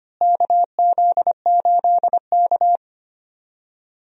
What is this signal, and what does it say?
Morse code, heard as a keyed tone, about 25 wpm, 700 Hz. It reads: KZ8K
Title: USA Call Signs - 4 Characters - Repeated 3x - 25wpm